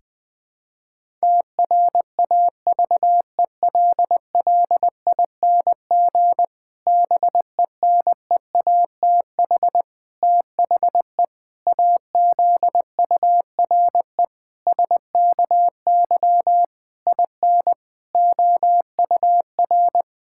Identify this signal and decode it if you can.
Morse code, heard as a keyed tone, about 20 wpm, 700 Hz. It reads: TRAVELLING BENEATH THE AZURE SKY IN OUR
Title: Pangram Challenge v2 - 20wpm